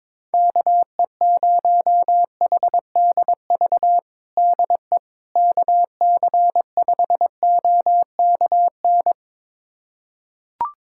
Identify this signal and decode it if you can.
Morse code, heard as a keyed tone, about 22 wpm, 700 Hz. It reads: KE0HDV DE KC5OKN
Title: USA Callsign DE Callsign 22wpm